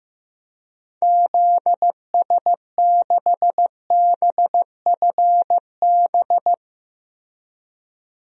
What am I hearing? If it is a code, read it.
Morse code, heard as a keyed tone, about 15 wpm, 700 Hz. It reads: ZS6BFB
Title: DXCC-100 Call Signs - 6 Characters - Repeated 2x - 15wpm